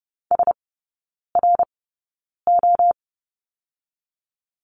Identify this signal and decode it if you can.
Morse code, heard as a keyed tone, about 30 wpm, spaced as for 9 wpm, 700 Hz. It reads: SRO
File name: course / 30:10wpm / 087 Lesson 15e - Number 5 - Three Characters 30wpm